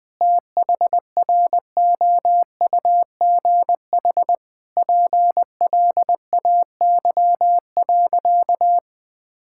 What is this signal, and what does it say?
Morse code, heard as a keyed tone, about 20 wpm, 700 Hz. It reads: THROUGH PLAY.
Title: Difficult Sentences with Hints - 20wpm